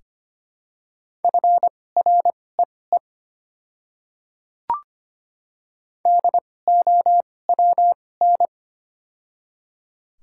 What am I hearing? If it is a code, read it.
Morse code, heard as a keyed tone, about 25 wpm, spaced as for 18 wpm, 700 Hz. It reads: FREE DOWN
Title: 2x Character Spacing - Top 500 Words - 25wpm